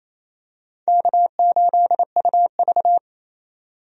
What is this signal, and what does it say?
Morse code, heard as a keyed tone, about 27 wpm, 700 Hz. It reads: K8UV
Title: USA Call Signs - 4 Characters - 28wpm